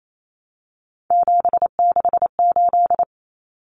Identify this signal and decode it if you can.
Morse code, heard as a keyed tone, about 28 wpm, 700 Hz. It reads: 768
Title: Common Cut Numbers - 3-Digit 28wpm